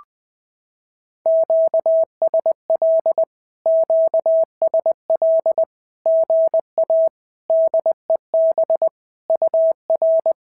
Morse code, heard as a keyed tone, about 20 wpm, 650 Hz. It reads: QSL QSL GA DEB UR
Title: SOTA QSO Practice 20wpm